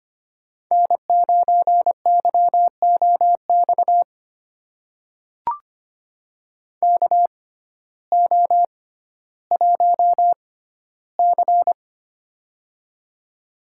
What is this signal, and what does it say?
Morse code, heard as a keyed tone, about 25 wpm, spaced as for 14 wpm, 700 Hz. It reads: N9YOX KO1C
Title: USA Call Signs - 6x Character Spacing - 25wpm